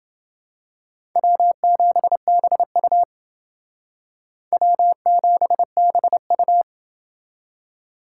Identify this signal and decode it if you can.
Morse code, heard as a keyed tone, about 28 wpm, 700 Hz. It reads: W7BU W7BU